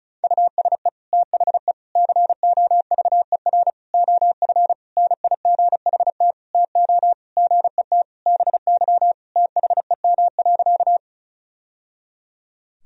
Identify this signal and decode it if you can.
Morse code, heard as a keyed tone, about 34 wpm, 700 Hz. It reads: USE THE COVER OF NIGHT TO GET BY THEM.
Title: Sentences from Top 1000 Words - Review Part 1 - 35wpm